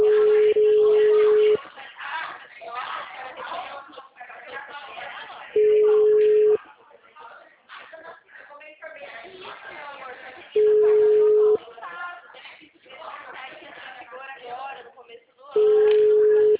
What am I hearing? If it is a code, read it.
Ringback tone (Europe, 425 Hz), with speech.